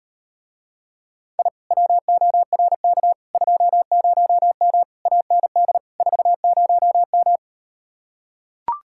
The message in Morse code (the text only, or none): I WORK 20M AND 40M